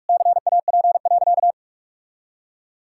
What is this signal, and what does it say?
Morse code, heard as a keyed tone, about 42 wpm, 700 Hz. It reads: CAP.
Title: Sentences from Top 1600 Words 45wpm